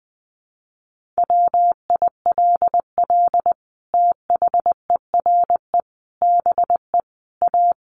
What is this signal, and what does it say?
Morse code, heard as a keyed tone, about 20 wpm, 700 Hz. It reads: WILL THERE BE A
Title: Sentences from Top 1400 Words 20wpm